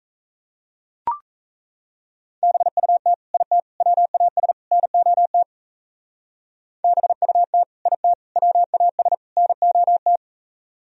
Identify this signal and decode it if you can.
Morse code, heard as a keyed tone, about 32 wpm, 700 Hz. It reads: BUT IT WAS NOT BUT IT WAS NOT
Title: Speed-Racing - Sets of 4 Words 28wpm